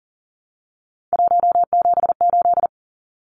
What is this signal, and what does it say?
Morse code, heard as a keyed tone, about 40 wpm, 700 Hz. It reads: 178
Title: Numbers - 3-Digit 40wpm